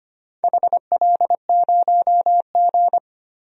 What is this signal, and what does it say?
Morse code, heard as a keyed tone, about 25 wpm, 700 Hz. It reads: HL0G
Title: DXCC-100 Call Signs - 4 Characters - Repeated 3x - 25wpm